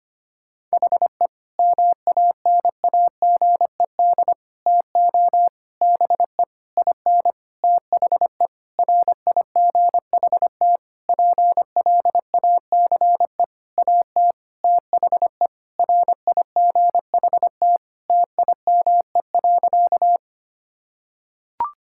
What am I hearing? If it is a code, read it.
Morse code, heard as a keyed tone, about 25 wpm, 700 Hz. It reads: HE MANAGED TO BE IN THE RIGHT PLACE AT THE RIGHT TIME.
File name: practice / 25wpm / Sentences with 9-Word Phrases - 25wpm